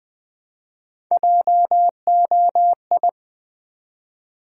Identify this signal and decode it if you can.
Morse code, heard as a keyed tone, about 20 wpm, 700 Hz. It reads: JOI
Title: Build A Word - 1 Letter At A Time - 3 to 5-Letter Words - 20wpm